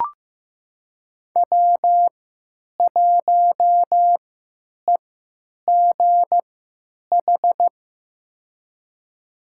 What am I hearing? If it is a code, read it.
Morse code, heard as a keyed tone, about 15 wpm, spaced as for 9 wpm, 700 Hz. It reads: W1EGH
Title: USA Call Signs - 5 Characters - 3x Character Spacing - 15wpm